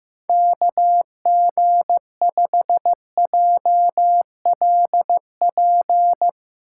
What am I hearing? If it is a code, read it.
Morse code, heard as a keyed tone, about 15 wpm, 700 Hz. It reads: KG5JLP
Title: USA Call Signs - 6 Characters - 3x Character Spacing - 15wpm